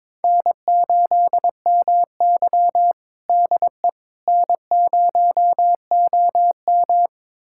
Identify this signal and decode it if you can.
Morse code, heard as a keyed tone, about 22 wpm, 700 Hz. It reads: N8MY DE N0OM
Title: USA Callsign DE Callsign - 4 Characters - 22wpm